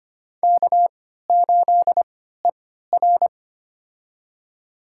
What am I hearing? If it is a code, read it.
Morse code, heard as a keyed tone, about 25 wpm, spaced as for 14 wpm, 700 Hz. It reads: K8ER
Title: USA Call Signs - 3x Character Spacing - 25wpm